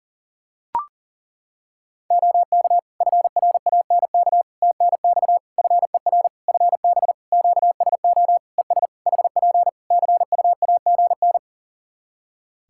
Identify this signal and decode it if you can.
Morse code, heard as a keyed tone, about 37 wpm, 700 Hz. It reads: OK FRANK TNX FER FB QSO ES HP CUAGN
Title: Practice QSO - Exchange 4 - Ending - 40wpm